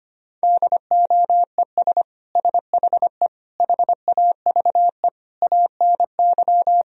Morse code, heard as a keyed tone, about 25 wpm, 700 Hz. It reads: DOES SHE HAVE ANY
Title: Sentences from Top 1000 Words - Review Part 3 - 25wpm